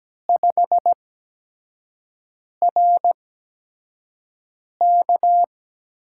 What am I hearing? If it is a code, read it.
Morse code, heard as a keyed tone, about 17 wpm, spaced as for 5 wpm, 700 Hz. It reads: 5RK
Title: USA Call Signs - 5 Characters - 8x Character Spacing - 17wpm